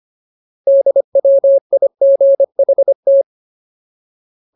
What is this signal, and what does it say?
Morse code, heard as a keyed tone, about 25 wpm, 550 Hz. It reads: DWIGHT
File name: Ultimate Word List / 25WPM Ultimate